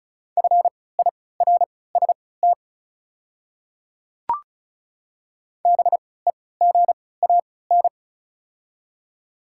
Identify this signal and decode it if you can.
Morse code, heard as a keyed tone, about 36 wpm, spaced as for 20 wpm, 690 Hz. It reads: FIRST BEGAN
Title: 3x Character Spacing - Top 300 Words - 35wpm